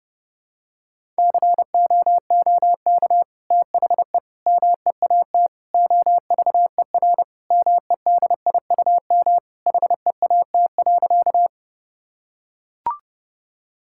Morse code, heard as a keyed tone, about 30 wpm, 700 Hz. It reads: COOK THE MEAT OVER MEDIUM HEAT.
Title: Sentences from Top 1000 Words - Review Part 6 - 30wpm